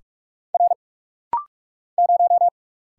Morse code, heard as a keyed tone, about 45 wpm, 700 Hz. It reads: R 0
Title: Single Letter-Number - Rapid-Fire 45wpm